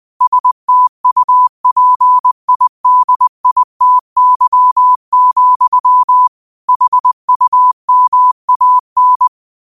Morse code, heard as a keyed tone, about 20 wpm, 1000 Hz. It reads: STUPIDITY, HUMAN
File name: qotdfiles / 20wpm